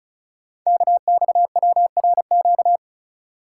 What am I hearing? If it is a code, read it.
Morse code, heard as a keyed tone, about 36 wpm, 690 Hz. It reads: KXWRQ
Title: Code Groups - 5 Letters - 35wpm